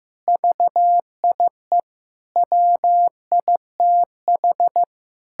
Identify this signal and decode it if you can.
Morse code, heard as a keyed tone, about 16 wpm, 700 Hz. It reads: VIE WITH
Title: Pangram Challenge v2 - 15wpm